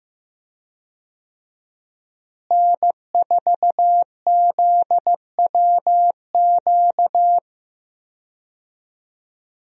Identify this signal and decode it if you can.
Morse code, heard as a keyed tone, about 15 wpm, 700 Hz. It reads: N4ZWQ
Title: USA Call Signs - 5 Characters - Repeated 3x - 15wpm